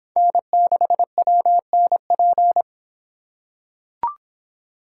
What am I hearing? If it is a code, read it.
Morse code, heard as a keyed tone, about 25 wpm, 700 Hz. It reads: N6WNP